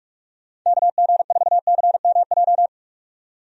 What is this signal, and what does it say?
Morse code, heard as a keyed tone, about 46 wpm, 700 Hz. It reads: KG4CMJ
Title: USA Call Signs - 6 Characters - 45wpm